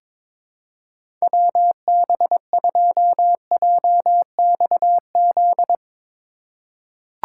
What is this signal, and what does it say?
Morse code, heard as a keyed tone, about 22 wpm, 700 Hz. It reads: WB2JXZ
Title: USA Call Signs - 6 Characters - 4x Character Spacing - 22wpm